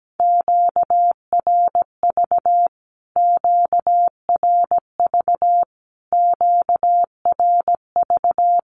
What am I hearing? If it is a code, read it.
Morse code, heard as a keyed tone, about 17 wpm, 700 Hz. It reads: QRV QRV QRV
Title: Q-Codes Repeated 3x 17wpm